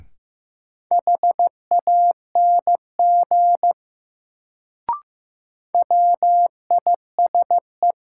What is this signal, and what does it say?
Morse code, heard as a keyed tone, about 15 wpm, 700 Hz. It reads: HANG WISE
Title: Sentences from Top 1600 Words - New Word Drill - 15wpm